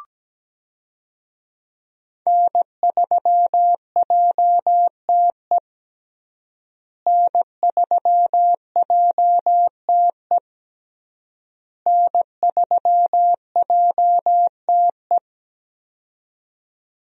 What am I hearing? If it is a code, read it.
Morse code, heard as a keyed tone, about 17 wpm, 700 Hz. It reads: N3JTE N3JTE N3JTE
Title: USA Call Signs - 5 Characters - Repeated 3x - 17wpm